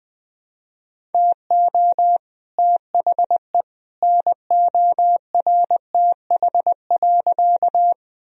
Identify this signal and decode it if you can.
Morse code, heard as a keyed tone, about 21 wpm, 700 Hz. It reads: TO THE NORTH.
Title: Sentences from Top 1000 Words - Review Part 1 - 20wpm